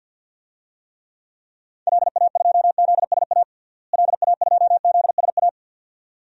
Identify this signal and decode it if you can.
Morse code, heard as a keyed tone, about 46 wpm, 700 Hz. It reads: LA2ZSA LA2ZSA
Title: DXCC-100 Call Signs - 6 Characters - Repeated 2x - 50wpm